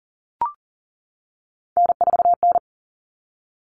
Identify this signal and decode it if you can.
Morse code, heard as a keyed tone, about 40 wpm, 700 Hz. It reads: N4N
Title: Common Cut Numbers - 3-Digit 40wpm